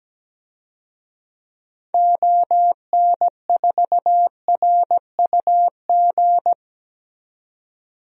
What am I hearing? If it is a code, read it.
Morse code, heard as a keyed tone, about 17 wpm, 700 Hz. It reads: ON4RUG
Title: DXCC-100 Call Signs - 6 Characters - 17wpm